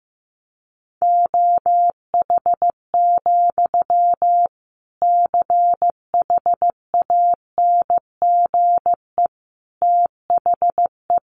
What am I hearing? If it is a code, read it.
Morse code, heard as a keyed tone, about 15 wpm, 700 Hz. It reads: OH, CHANGE THE
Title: Sentences from Top 800 Words 15wpm